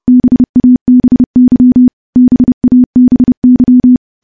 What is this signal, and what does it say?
Morse code, heard as a keyed tone, about 30 wpm, 250 Hz. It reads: BABY BABY
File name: morse / morse-training-30WPM